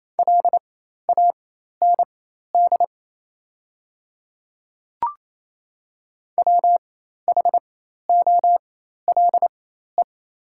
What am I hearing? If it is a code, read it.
Morse code, heard as a keyed tone, about 28 wpm, spaced as for 13 wpm, 700 Hz. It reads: LAND WHOLE
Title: 4x Character Spacing - Top 500 Words - 28wpm